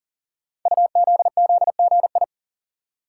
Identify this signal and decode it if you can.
Morse code, heard as a keyed tone, about 40 wpm, 700 Hz. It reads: UZZGI
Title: Code Groups - 5 Letters - 40wpm